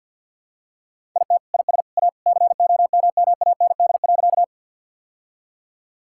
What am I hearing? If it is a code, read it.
Morse code, heard as a keyed tone, about 50 wpm, 700 Hz. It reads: IT IS A COMMAND.
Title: Sentences from Top 1600 Words 50wpm